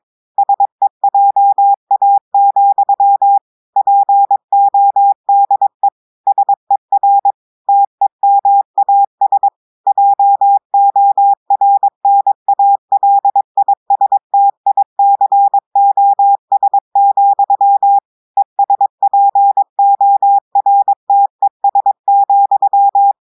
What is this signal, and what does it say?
Morse code, heard as a keyed tone, about 22 wpm, 800 Hz. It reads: SEJA, PODE SER TEMAS JORNALISTICOS, ESPORTES,
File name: audios / portugais / portugais texte en clair 2 vitesse 22 mots minutes